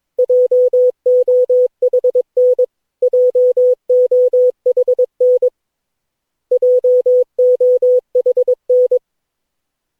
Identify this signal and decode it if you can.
Morse code, heard as a keyed tone, about 22 wpm, 500 Hz. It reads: JOHN JOHN JOHN
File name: cw / Session9 cw training